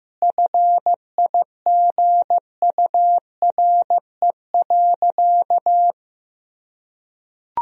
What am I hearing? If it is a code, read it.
Morse code, heard as a keyed tone, about 16 wpm, 700 Hz. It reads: FIGURE.
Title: Sentences from Top 1000 Words - Review Part 2 - 15wpm